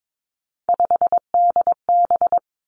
Morse code, heard as a keyed tone, about 22 wpm, 700 Hz. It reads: 5DB
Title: Three Challenging Letter-Number Combinations 22wpm